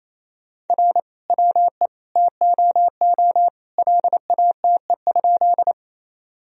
Morse code, heard as a keyed tone, about 27 wpm, 700 Hz. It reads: R WE TOO LATE?
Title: Sentences from Top 300 Words and Abbreviations 28wpm